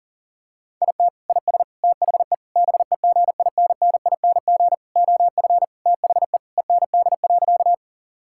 Morse code, heard as a keyed tone, about 37 wpm, 700 Hz. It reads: IT IS THE BEGINNING OF THE END.
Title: Sentences from Top 1000 Words - Review Part 5 - 40wpm